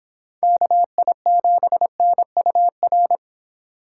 Morse code, heard as a keyed tone, about 32 wpm, 700 Hz. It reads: KI7NUR